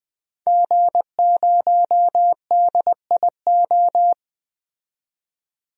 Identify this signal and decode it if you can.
Morse code, heard as a keyed tone, about 21 wpm, 700 Hz. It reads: G0DIO
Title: DXCC-100 Call Signs - 5 Characters - 20wpm